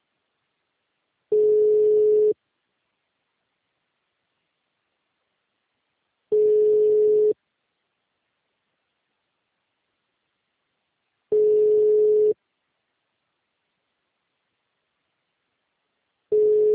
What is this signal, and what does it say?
Ringback tone (Europe, 425 Hz).